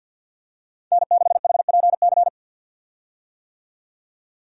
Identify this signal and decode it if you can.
Morse code, heard as a keyed tone, about 50 wpm, 700 Hz. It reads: N6HPC
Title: USA Call Signs - 5 Characters - 50wpm